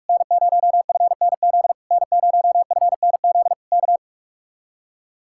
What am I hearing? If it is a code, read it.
Morse code, heard as a keyed tone, about 58 wpm, 700 Hz. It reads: N0FNZ N0FNZ K